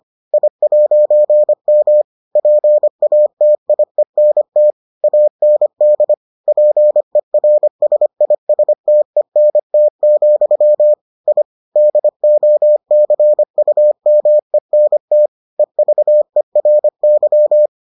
Morse code, heard as a keyed tone, about 25 wpm, 600 Hz. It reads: I'M PATIENT AND PERSISTENT, I DOCUMENT EVERY